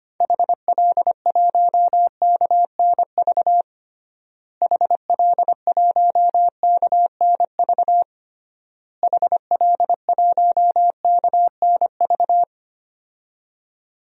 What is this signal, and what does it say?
Morse code, heard as a keyed tone, about 25 wpm, 700 Hz. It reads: HL1KNV HL1KNV HL1KNV
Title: DXCC-100 Call Signs - 6 Characters - Repeated 3x - 25wpm